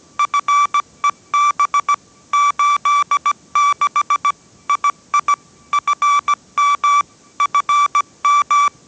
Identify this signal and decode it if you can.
Morse code, heard as a keyed tone, about 17 wpm, 1160 Hz. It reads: FEB 86 II FM FM